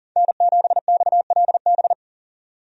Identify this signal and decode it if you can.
Morse code, heard as a keyed tone, about 40 wpm, 700 Hz. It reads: N7XLB
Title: USA Call Signs - 5 Characters - 40wpm